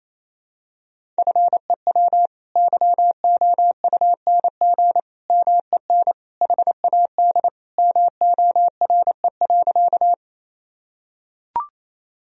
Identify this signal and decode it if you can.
Morse code, heard as a keyed tone, about 28 wpm, 700 Hz. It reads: FEW YOUNG MEN HAD MORE.
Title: Sentences from Top 1000 Words - Review Part 1 - 28wpm